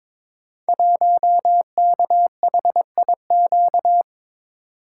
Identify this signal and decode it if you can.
Morse code, heard as a keyed tone, about 22 wpm, 700 Hz. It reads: 1KHIQ
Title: Code Groups - 5 Letter:Numbers - 22wpm